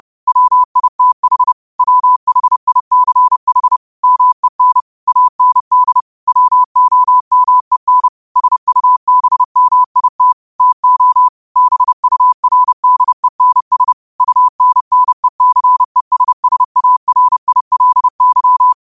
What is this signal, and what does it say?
Morse code, heard as a keyed tone, about 30 wpm, 1000 Hz. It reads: WITH WHICH MEN AND WOMEN SUBMIT TO BURDENS UNNECESSARILY
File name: qotdfiles / 30wpm